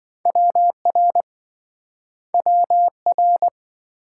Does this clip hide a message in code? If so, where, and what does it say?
Yes: Morse code from 0.3 to 3.5 s, heard as a keyed tone, about 25 wpm, 700 Hz. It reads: WR WR